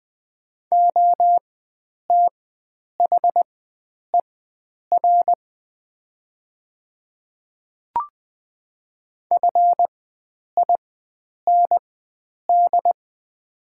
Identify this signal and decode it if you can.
Morse code, heard as a keyed tone, about 20 wpm, spaced as for 9 wpm, 700 Hz. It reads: OTHER FIND
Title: 4x Character Spacing - Top 100 Words - 20wpm